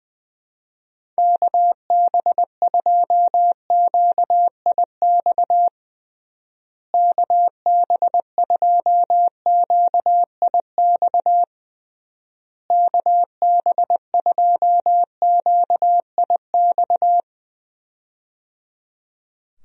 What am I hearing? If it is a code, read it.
Morse code, heard as a keyed tone, about 20 wpm, 700 Hz. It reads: KB2QIX KB2QIX KB2QIX
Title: USA Call Signs - 6 Characters - Repeated 3x - 20wpm